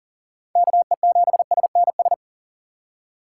Morse code, heard as a keyed tone, about 40 wpm, 700 Hz. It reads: KE7SNS
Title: USA Call Signs - 6 Characters - 2x Character Spacing - 40wpm